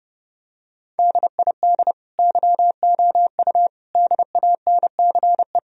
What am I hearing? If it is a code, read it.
Morse code, heard as a keyed tone, about 32 wpm, 700 Hz. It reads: DID YOU DANCE
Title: Sentences from Top 1000 Words - Review Part 4 - 30wpm